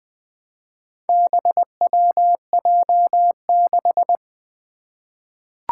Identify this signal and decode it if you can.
Morse code, heard as a keyed tone, about 20 wpm, 700 Hz. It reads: BWJ6
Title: Four Challenging Letter-Number Combinations 20wpm